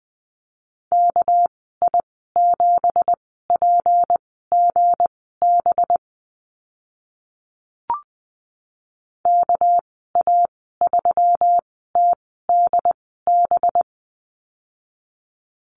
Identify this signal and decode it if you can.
Morse code, heard as a keyed tone, about 19 wpm, spaced as for 14 wpm, 700 Hz. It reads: KI7PGB KA3TDB
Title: 219 Lesson 35g - Number 0 - Call Signs 20wpm